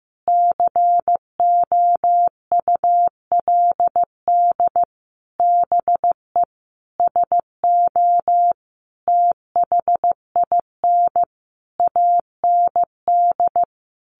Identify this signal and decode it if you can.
Morse code, heard as a keyed tone, about 15 wpm, 700 Hz. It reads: COULD BE SO THIN AND
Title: Sentences from Top 1200 Words 15wpm